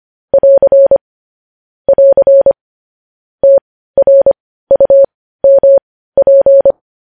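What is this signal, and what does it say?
Morse code, heard as a keyed tone, about 25 wpm, spaced as for 15 wpm, 550 Hz. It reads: <AR> <AR> TRUMP
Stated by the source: News headlines sent as Morse practice at 15.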